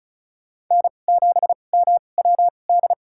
Morse code, heard as a keyed tone, about 34 wpm, spaced as for 25 wpm, 690 Hz. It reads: N7MWD
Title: USA Call Signs - 5 Characters - 2x Character Spacing - 35wpm